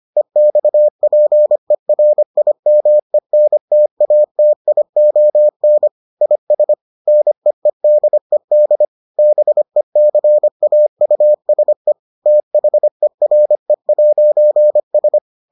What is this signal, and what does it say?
Morse code, heard as a keyed tone, about 25 wpm, 600 Hz. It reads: EXPERIMENTATION IS NEEDED BECAUSE THERE'S